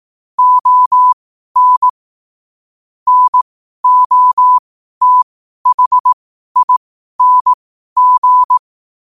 Morse code, heard as a keyed tone, about 18 wpm, spaced as for 12 wpm, 1000 Hz. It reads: ON NOTHING
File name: qotdfiles / 12wpm